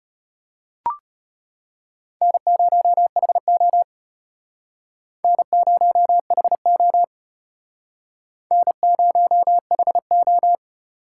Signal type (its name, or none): Morse code